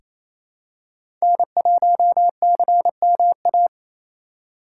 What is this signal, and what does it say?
Morse code, heard as a keyed tone, about 27 wpm, 700 Hz. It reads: N1CMA
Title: USA Call Signs - 5 Characters - 6x Character Spacing - 28wpm